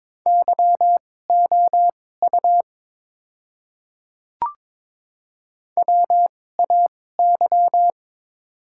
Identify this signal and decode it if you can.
Morse code, heard as a keyed tone, about 22 wpm, spaced as for 16 wpm, 700 Hz. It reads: YOU WAY
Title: 2x Character Spacing - Top 100 Words - 22wpm